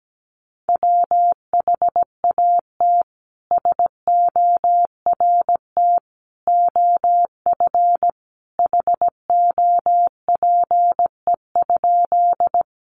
Morse code, heard as a keyed tone, about 17 wpm, 700 Hz. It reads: WHAT SORT OF HOPE?
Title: Sentences from Top 1300 Words 17wpm